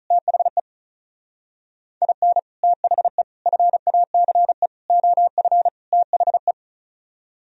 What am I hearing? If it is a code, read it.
Morse code, heard as a keyed tone, about 43 wpm, 700 Hz. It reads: THE IN THE FACE OF THE
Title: Speed-Racing - Sets of 5 Words 35wpm